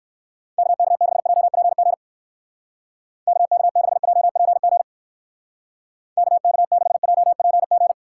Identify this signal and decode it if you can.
Morse code, heard as a keyed tone, about 57 wpm, 700 Hz. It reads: KK6JPG KK6JPG KK6JPG